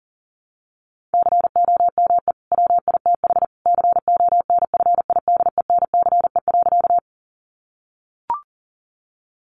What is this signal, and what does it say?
Morse code, heard as a keyed tone, about 37 wpm, 700 Hz. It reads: COME WITH CONFIDENCE.
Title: Sentences from Top 1500 Words 40wpm